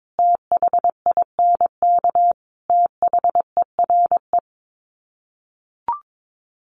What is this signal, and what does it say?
Morse code, heard as a keyed tone, about 22 wpm, 700 Hz. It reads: THINK THERE
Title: Sets of 5 Words 22wpm